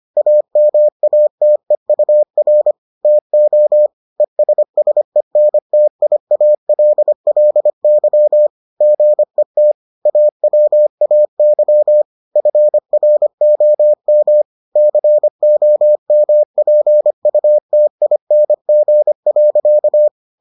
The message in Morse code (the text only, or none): AMATEUR TO ESSENTIALLY GET AWAY FROM COMPUTING.